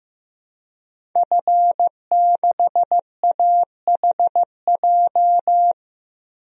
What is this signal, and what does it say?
Morse code, heard as a keyed tone, about 15 wpm, 700 Hz. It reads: F6AHJ
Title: DXCC-100 Call Signs - 5 Characters - 15wpm